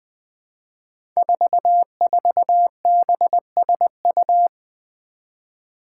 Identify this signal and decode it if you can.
Morse code, heard as a keyed tone, about 20 wpm, 700 Hz. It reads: 44BSU
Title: Five Challenging Letter-Number Combinations 20wpm